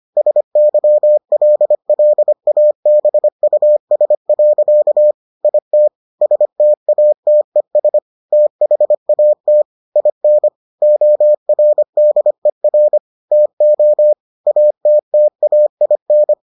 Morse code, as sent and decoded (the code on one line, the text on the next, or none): ... -.-- .-.. .-.. .- -... ..- ... .-.-.- / .. - / ... - .- - . ... / - .... .- - / .. -. / --- .-. -.. .-. / - --- / .- - - .- .. -.
SYLLABUS. IT STATES THAT IN ORDER TO ATTAIN